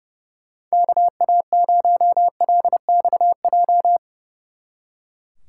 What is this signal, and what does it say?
Morse code, heard as a keyed tone, about 35 wpm, 700 Hz. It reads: KA0LXJ
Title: Speed Racing - USA 2x3 Call Signs 30wpm